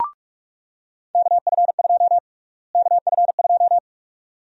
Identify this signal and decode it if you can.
Morse code, heard as a keyed tone, about 46 wpm, 700 Hz. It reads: KF2 KF2
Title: USA Call Sign Prefixes - Repeated 2x - 45wpm